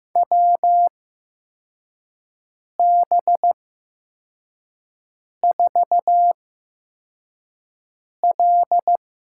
Morse code, heard as a keyed tone, about 15 wpm, spaced as for 4 wpm, 700 Hz. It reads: WB4L
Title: USA Call Signs - 8x Character Spacing - 15wpm